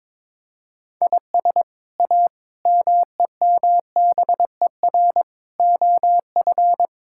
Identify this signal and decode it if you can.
Morse code, heard as a keyed tone, about 22 wpm, 700 Hz. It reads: IS A MEMBER OF
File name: practice / 22wpm / Build A Phrase - 5-Word Phrases - 22wpm